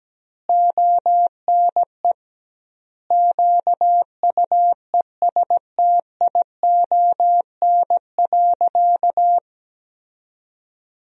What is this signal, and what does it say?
Morse code, heard as a keyed tone, about 17 wpm, 700 Hz. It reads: ONE QUESTION.
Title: 2x Word-Spacing - Sentences from Top 400 Words - 17wpm